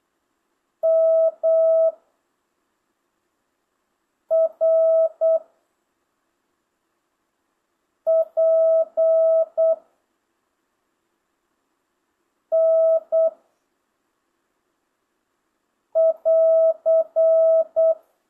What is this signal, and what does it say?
Morse code, heard as a keyed tone, about 8 wpm, 660 Hz. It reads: M R P N <AR>
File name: jh8cbh / 2025cw